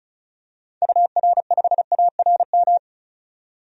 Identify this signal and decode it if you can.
Morse code, heard as a keyed tone, about 34 wpm, 700 Hz. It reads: UR5ARM
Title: DXCC-100 Call Signs - 6 Characters - 35wpm